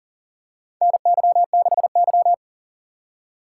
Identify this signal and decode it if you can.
Morse code, heard as a keyed tone, about 40 wpm, 700 Hz. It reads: NY6Y
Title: K1USN SST QSO Practice - 40wpm